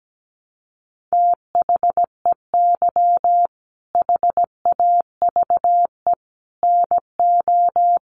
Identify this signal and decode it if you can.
Morse code, heard as a keyed tone, about 17 wpm, 700 Hz. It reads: THEY HAVE NO
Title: Sentences from Top 1300 Words 17wpm